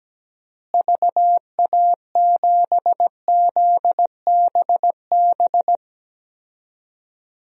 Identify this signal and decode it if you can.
Morse code, heard as a keyed tone, about 17 wpm, 700 Hz. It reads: VA7ZBB
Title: DXCC-100 Call Signs - 6 Characters - Repeated 2x - 17wpm